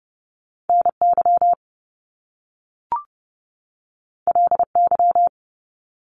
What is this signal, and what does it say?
Morse code, heard as a keyed tone, about 30 wpm, 700 Hz. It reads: NY LY
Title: Top Two Letter Combinations in Words 30wpm